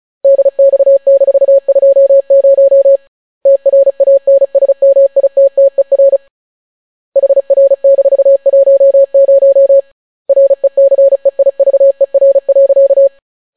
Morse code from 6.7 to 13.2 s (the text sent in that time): HR-10 RECEIVER.